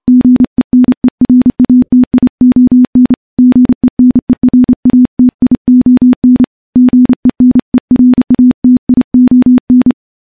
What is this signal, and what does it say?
Morse code, heard as a keyed tone, about 28 wpm, 250 Hz. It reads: GENERATION GENERATION GENERATION
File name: morse / morse-training-30WPM